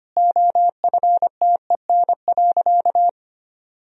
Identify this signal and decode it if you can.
Morse code, heard as a keyed tone, about 25 wpm, 700 Hz. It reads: OFTEN.
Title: Sentences from Top 1000 Words - Review Part 1 - 25wpm